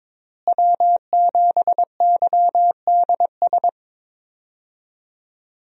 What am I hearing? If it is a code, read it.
Morse code, heard as a keyed tone, about 22 wpm, 700 Hz. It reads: W7YDS
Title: USA Call Signs - 5 Characters - Repeated 3x - 22wpm